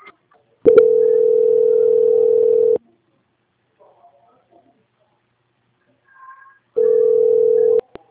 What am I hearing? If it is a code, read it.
Ringback tone (North America), 440 + 480 Hz.
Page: Boa Viagem - RE